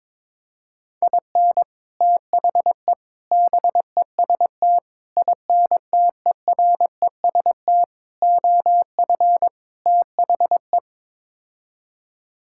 Morse code, heard as a keyed tone, about 22 wpm, 700 Hz. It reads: IN THE BEST INTEREST OF THE
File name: practice / 22wpm / Sets of 6 Words - 22wpm